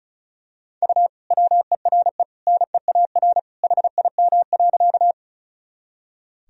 Morse code, heard as a keyed tone, about 35 wpm, 700 Hz. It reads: U WERE NEAR HIM.
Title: Sentences from Top 200 Words and Abbreviations 35wpm